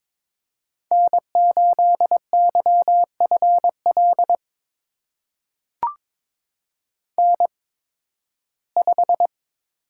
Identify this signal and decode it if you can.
Morse code, heard as a keyed tone, about 22 wpm, spaced as for 14 wpm, 700 Hz. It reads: N8YFL N5
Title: USA Call Signs - 8x Character Spacing - 22wpm